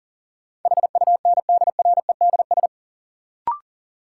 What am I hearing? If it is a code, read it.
Morse code, heard as a keyed tone, about 40 wpm, 700 Hz. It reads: HUNDREDS
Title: Sets of 4 Words - New Word Drill - 40wpm